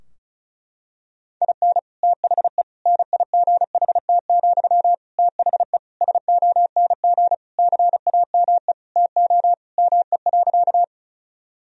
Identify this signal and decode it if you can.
Morse code, heard as a keyed tone, about 34 wpm, 700 Hz. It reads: IN THE NIGHT, THE SONG CAME TO ME.
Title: Sentences from Top 1000 Words - Review Part 2 - 35wpm